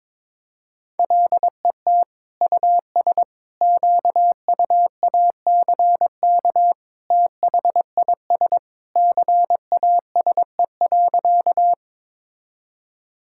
Morse code, heard as a keyed tone, about 22 wpm, 700 Hz. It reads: LET US QUACK THIS CASE.
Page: morse-code-ninja-practice-22wpm